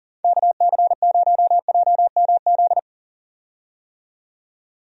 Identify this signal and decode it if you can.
Morse code, heard as a keyed tone, about 40 wpm, 700 Hz. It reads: KC0JMZ
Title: USA Call Signs - 6 Characters - 40wpm